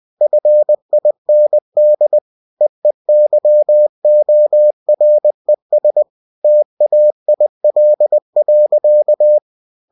Morse code, heard as a keyed tone, about 20 wpm, 600 Hz. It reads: FIND EEYORES TAIL.